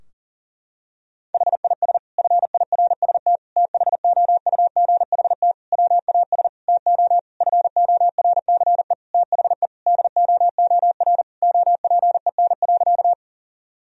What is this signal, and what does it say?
Morse code, heard as a keyed tone, about 37 wpm, 700 Hz. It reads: HIS FIRST THOUGHT WAS TO FORCE THE DOOR OPEN.
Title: Sentences from Top 1000 Words - Review Part 3 - 40wpm